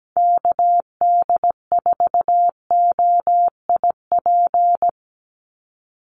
Morse code, heard as a keyed tone, about 17 wpm, 700 Hz. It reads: KD4OIP
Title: USA Call Signs 17wpm